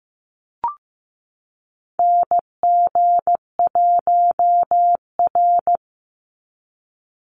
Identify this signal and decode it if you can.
Morse code, heard as a keyed tone, about 15 wpm, 700 Hz. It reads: NG1R
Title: Call Signs Repeated 2x - 15wpm